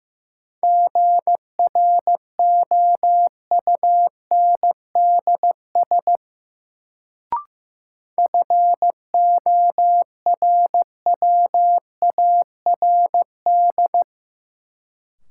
Morse code, heard as a keyed tone, about 15 wpm, 700 Hz. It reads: GROUNDS FORWARD
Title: Sentences from Top 1300 Words - New Word Drill - 15wpm